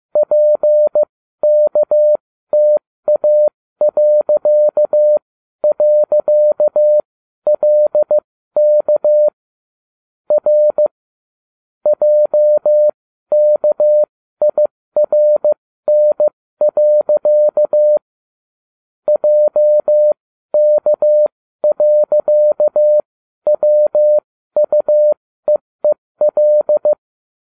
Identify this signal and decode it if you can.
Morse code, heard as a keyed tone, about 15 wpm, spaced as for 12 wpm, 600 Hz. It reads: PKTA..LK R JKIRN. JK.WUEEL